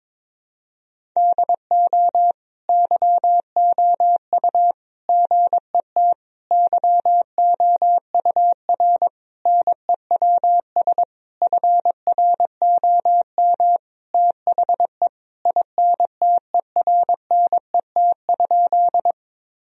Morse code, heard as a keyed tone, about 22 wpm, 700 Hz. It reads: DO YOU GET YOUR NEWS FROM THE INTERNET?
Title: Sentences from Top 1000 Words - Review Part 4 - 22wpm